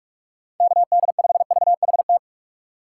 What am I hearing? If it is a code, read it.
Morse code, heard as a keyed tone, about 47 wpm, 700 Hz. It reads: KD5VHT
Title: USA Call Signs - 6 Characters - 4x Character Spacing - 45wpm